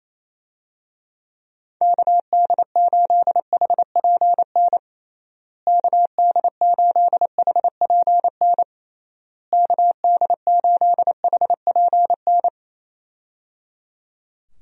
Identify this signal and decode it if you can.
Morse code, heard as a keyed tone, about 28 wpm, 700 Hz. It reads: KD8HPN KD8HPN KD8HPN
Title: USA Call Signs - 6 Characters - Repeated 3x - 28wpm